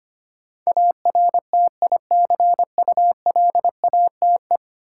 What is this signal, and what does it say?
Morse code, heard as a keyed tone, about 25 wpm, 700 Hz. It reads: ARTICULATE
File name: 10 Letter Words - First 3 Letters Given - 25wpm